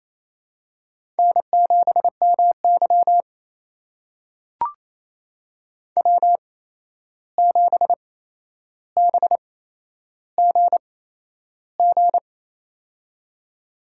Morse code, heard as a keyed tone, about 28 wpm, spaced as for 11 wpm, 700 Hz. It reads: N7MY W7BGG
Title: USA Call Signs - 8x Character Spacing - 28wpm